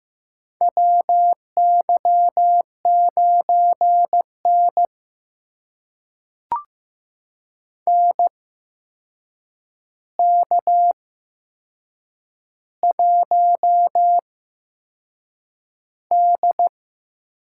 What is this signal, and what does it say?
Morse code, heard as a keyed tone, about 15 wpm, spaced as for 6 wpm, 700 Hz. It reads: WY9N NK1D
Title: USA Call Signs - 4 Characters - 8x Character Spacing - 15wpm